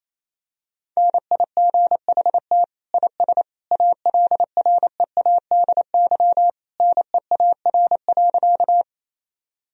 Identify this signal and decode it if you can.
Morse code, heard as a keyed tone, about 28 wpm, 700 Hz. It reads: NIGHT IS ALREADY NEAR.
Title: Sentences from Top 1000 Words - Review Part 1 - 28wpm